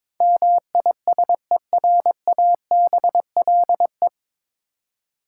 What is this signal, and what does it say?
Morse code, heard as a keyed tone, about 22 wpm, 700 Hz. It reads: MISERABLE
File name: 9 Letter Words - First 3 Letters Given - 22wpm